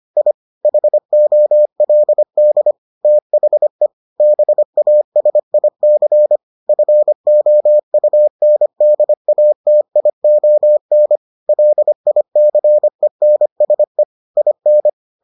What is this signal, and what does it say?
Morse code, heard as a keyed tone, about 25 wpm, 600 Hz. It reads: I HOLD THE BASIC FOUNDATION LICENSE IN